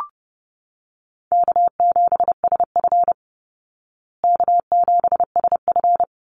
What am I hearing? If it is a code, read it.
Morse code, heard as a keyed tone, about 30 wpm, 700 Hz. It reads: K7SF K7SF
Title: Easy Call Signs Repeated 2x - 30wpm